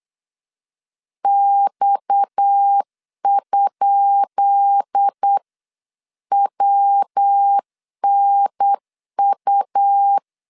Morse code, heard as a keyed tone, about 8 wpm, 800 Hz. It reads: X?WNU
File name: LEZIONI / 10 SIGNS / 10B INT 8wpm 3534